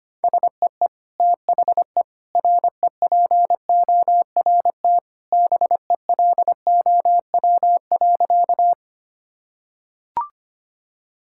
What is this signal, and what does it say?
Morse code, heard as a keyed tone, about 25 wpm, 700 Hz. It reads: SEE THE REPORT BELOW.
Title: Sentences from Top 1000 Words - Review Part 6 - 25wpm